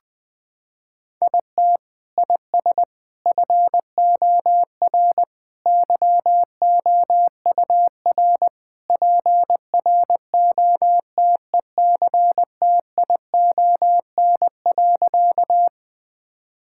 Morse code, heard as a keyed tone, about 20 wpm, 700 Hz. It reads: IT IS FOR YOUR PROTECTION.
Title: Sentences from Top 1600 Words 20wpm